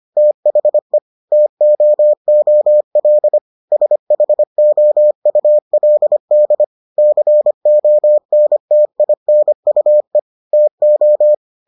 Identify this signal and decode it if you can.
Morse code, heard as a keyed tone, about 25 wpm, 600 Hz. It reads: THE TOOL SHOULD CONTINUE TO